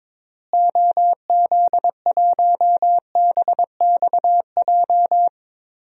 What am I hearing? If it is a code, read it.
Morse code, heard as a keyed tone, about 22 wpm, 700 Hz. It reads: OZ1BXJ
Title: DXCC-100 Call Signs - 6 Characters - Repeated 3x - 22wpm